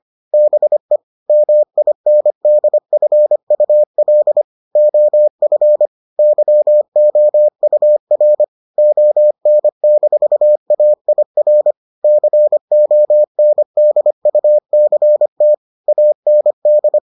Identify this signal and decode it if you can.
Morse code, heard as a keyed tone, about 25 wpm, 600 Hz. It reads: BE MINDFUL OF YOUR ON-AIR CONDUCT AND